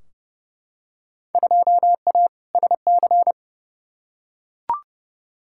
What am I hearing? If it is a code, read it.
Morse code, heard as a keyed tone, about 32 wpm, 700 Hz. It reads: 2A SC
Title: Field Day Exchanges - 30wpm